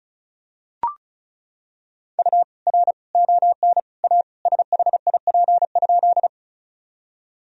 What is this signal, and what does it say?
Morse code, heard as a keyed tone, about 36 wpm, 700 Hz. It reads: U R ON A SHIP?
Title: Sentences from Top 400 Words and Abbreviations 35wpm